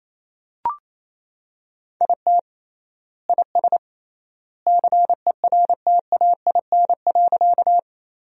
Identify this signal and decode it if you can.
Morse code, heard as a keyed tone, about 28 wpm, 700 Hz. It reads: IT IS CERTAIN.
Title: 3x Word-Spacing - Sentences from Top 500 Words - 28wpm